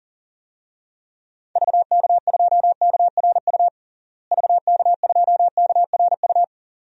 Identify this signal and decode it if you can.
Morse code, heard as a keyed tone, about 40 wpm, 700 Hz. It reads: VK2KRU VK2KRU
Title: DXCC-100 Call Signs - 6 Characters - Repeated 2x - 40wpm